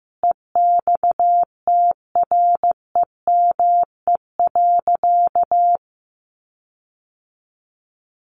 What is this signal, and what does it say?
Morse code, heard as a keyed tone, about 15 wpm, 700 Hz. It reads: EXTREME.
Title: Sentences from Top 1300 Words 15wpm